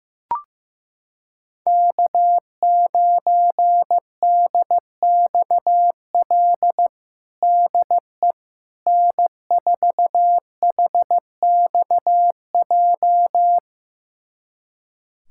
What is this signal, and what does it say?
Morse code, heard as a keyed tone, about 15 wpm, 700 Hz. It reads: K9DXL DE N4HXJ
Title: USA Callsign DE Callsign - 5 Characters - 15wpm